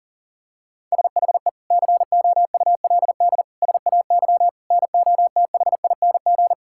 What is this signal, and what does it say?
Morse code, heard as a keyed tone, about 40 wpm, 700 Hz. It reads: SHE COULD SAY NOTHING
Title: Sentences from Top 1600 Words 40wpm